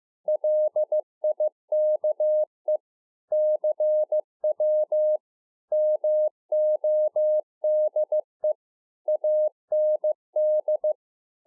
Morse code, heard as a keyed tone, about 15 wpm, 610 Hz. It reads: LIKE CW MODE AND